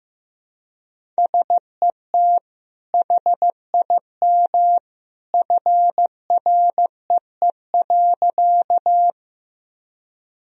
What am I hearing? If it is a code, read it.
Morse code, heard as a keyed tone, about 16 wpm, 700 Hz. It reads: SET HIM FREE.
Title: Sentences from Top 1000 Words - Review Part 1 - 15wpm